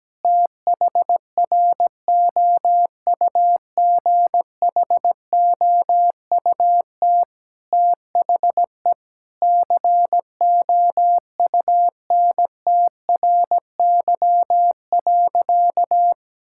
Morse code, heard as a keyed tone, about 17 wpm, 700 Hz. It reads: THROUGHOUT THE COUNTRY.
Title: Sentences from Top 1000 Words - Review Part 5 - 17wpm